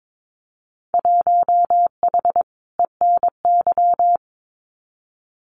Morse code, heard as a keyed tone, about 22 wpm, 700 Hz. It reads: 1H ENY
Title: Winter Field Day Exchanges 22wpm